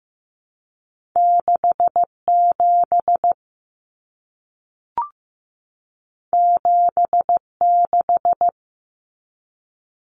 Morse code, heard as a keyed tone, about 15 wpm, 700 Hz. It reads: 67 76
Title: Common Cut Numbers - 2-Digit 15wpm